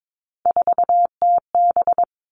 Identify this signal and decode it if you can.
Morse code, heard as a keyed tone, about 22 wpm, 700 Hz. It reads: 4TB
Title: Three Challenging Letter-Number Combinations 22wpm